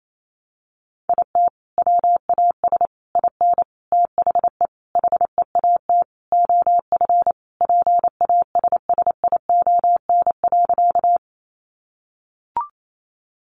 Morse code, heard as a keyed tone, about 28 wpm, 700 Hz. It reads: IT WAS IN THE HEAT OF PASSION.
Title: Sentences from Top 1300 Words 28wpm